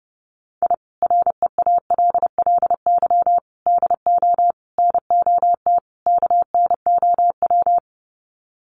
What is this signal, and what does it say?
Morse code, heard as a keyed tone, about 30 wpm, 700 Hz. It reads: I REALLY DO NOT KNOW
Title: Sets of 5 Words Encore 30wpm